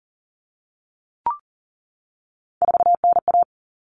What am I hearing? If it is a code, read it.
Morse code, heard as a keyed tone, about 40 wpm, 700 Hz. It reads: 4NA
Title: Common Cut Numbers - 3-Digit 40wpm